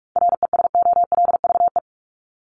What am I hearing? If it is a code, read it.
Morse code, heard as a keyed tone, about 44 wpm, 700 Hz. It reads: RESOLVE
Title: Top 1751-2000 Words 45wpm